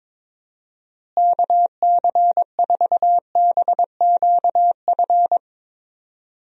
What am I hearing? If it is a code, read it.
Morse code, heard as a keyed tone, about 22 wpm, 700 Hz. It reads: KC4BQF
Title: USA Call Signs - 6 Characters - Repeated 3x - 22wpm